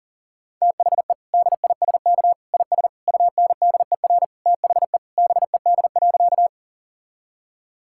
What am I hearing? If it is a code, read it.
Morse code, heard as a keyed tone, about 40 wpm, 700 Hz. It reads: THE DISK IS UNDER THE BED.
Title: Sentences from Top 1000 Words - Review Part 6 - 40wpm